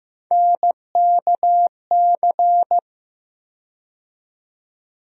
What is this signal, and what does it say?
Morse code, heard as a keyed tone, about 15 wpm, 700 Hz. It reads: NKC
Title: USA Call Sign Postfixes - 3 Letters 15wpm